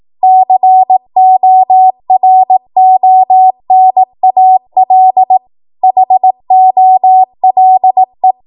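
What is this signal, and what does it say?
Morse code, heard as a keyed tone, about 18 wpm, 750 Hz. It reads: CORONAL HOLE
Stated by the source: 18 WPM Morse code practice from a national amateur radio association's code archive.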